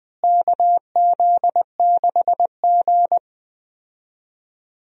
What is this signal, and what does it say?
Morse code, heard as a keyed tone, about 20 wpm, 700 Hz. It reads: KZ6G
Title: USA Call Signs - 4 Characters - Repeated 2x - 20wpm